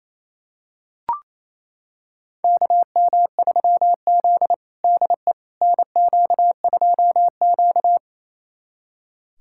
Morse code, heard as a keyed tone, about 27 wpm, 700 Hz. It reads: KM3Z DE NQ2Q
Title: USA Callsign DE Callsign - 4 Characters - 28wpm